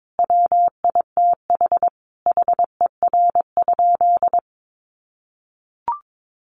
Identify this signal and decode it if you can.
Morse code, heard as a keyed tone, about 22 wpm, 700 Hz. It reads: WITH HER?
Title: Sentences from Top 600 Words 22wpm